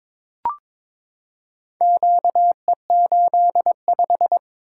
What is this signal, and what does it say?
Morse code, heard as a keyed tone, about 22 wpm, 700 Hz. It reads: QE85
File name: practice / 22wpm / Four Letter-Number Combinations - 22wpm